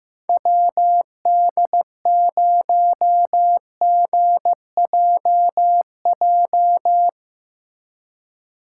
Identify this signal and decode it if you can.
Morse code, heard as a keyed tone, about 17 wpm, 700 Hz. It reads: WD0GJJ
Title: Speed Racing - USA 2x3 Call Signs 15wpm